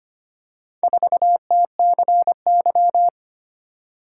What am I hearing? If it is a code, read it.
Morse code, heard as a keyed tone, about 25 wpm, 700 Hz. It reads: 4TCY
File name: practice / 25wpm / Four Letter-Number Combinations - 25wpm